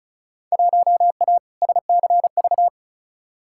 Morse code, heard as a keyed tone, about 36 wpm, 700 Hz. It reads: 1A SCV
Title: Field Day Exchanges - 35wpm